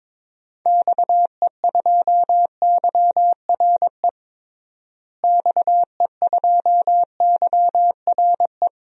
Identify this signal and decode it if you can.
Morse code, heard as a keyed tone, about 22 wpm, 700 Hz. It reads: XE2YRE XE2YRE
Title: DXCC-100 Call Signs - 6 Characters - Repeated 2x - 22wpm